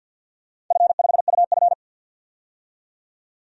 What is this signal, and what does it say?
Morse code, heard as a keyed tone, about 45 wpm, 700 Hz. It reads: F5UF
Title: DXCC-100 Call Signs - 4 Characters - 50wpm